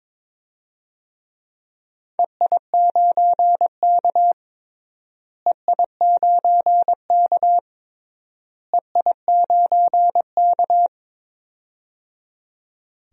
Morse code, heard as a keyed tone, about 22 wpm, 700 Hz. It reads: EI9K EI9K EI9K
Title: DXCC-100 Call Signs - 4 Characters - Repeated 3x - 22wpm